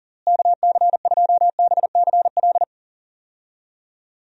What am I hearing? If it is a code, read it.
Morse code, heard as a keyed tone, about 40 wpm, 700 Hz. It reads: KC2BCL
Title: USA Call Signs - 6 Characters - Repeated 2x - 40wpm